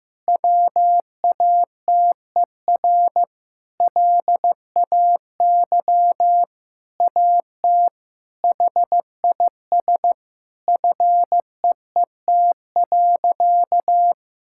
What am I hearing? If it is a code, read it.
Morse code, heard as a keyed tone, about 15 wpm, 700 Hz. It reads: WATER LAY AT HIS FEET.
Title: Sentences from Top 1000 Words - Review Part 5 - 15wpm